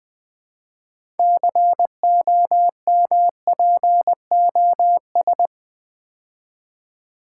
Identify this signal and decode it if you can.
Morse code, heard as a keyed tone, about 20 wpm, 700 Hz. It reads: COMPOS
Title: Build A Word - 2 Letters At A Time - 8-Letter Words - 20wpm